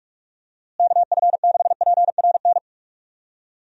Morse code, heard as a keyed tone, about 46 wpm, 700 Hz. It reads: KF6PRN
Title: USA Call Signs - 6 Characters - Repeated 2x - 45wpm